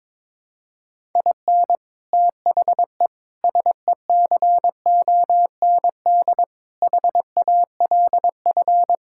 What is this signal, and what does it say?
Morse code, heard as a keyed tone, about 22 wpm, 700 Hz. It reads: IN THE SECOND HALF
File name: practice / 22wpm / Build A Phrase - 5-Word Phrases - 22wpm